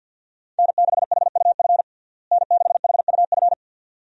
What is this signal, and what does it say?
Morse code, heard as a keyed tone, about 46 wpm, 700 Hz. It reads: N6HUF N6HUF
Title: USA Call Signs - 5 Characters - Repeated 2x - 50wpm